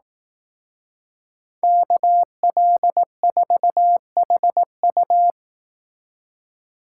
Morse code, heard as a keyed tone, about 18 wpm, 700 Hz. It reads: KL4HU
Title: Speed Racing - USA 2x2 Call Signs 15wpm